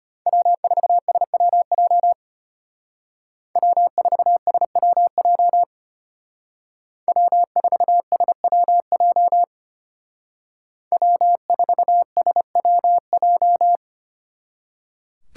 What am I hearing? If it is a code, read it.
Morse code, heard as a keyed tone, about 33 wpm, 700 Hz. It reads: W4SWJ W4SWJ W4SWJ W4SWJ